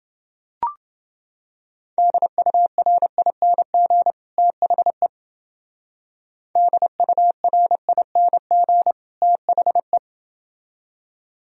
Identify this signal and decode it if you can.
Morse code, heard as a keyed tone, about 28 wpm, 700 Hz. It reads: DURING THE DURING THE